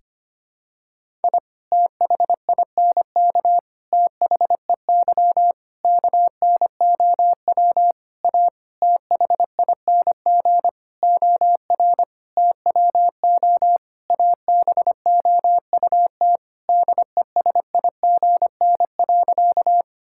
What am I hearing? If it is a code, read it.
Morse code, heard as a keyed tone, about 25 wpm, 700 Hz. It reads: I THINK THEY KNOW A THING OR TWO ABOUT DESIGN.
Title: Sentences from Top 1000 Words - Review Part 5 - 25wpm